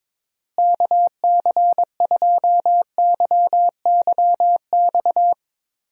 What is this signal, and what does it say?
Morse code, heard as a keyed tone, about 22 wpm, 700 Hz. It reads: KC2YYX
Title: USA Call Signs - 6 Characters - Repeated 2x - 22wpm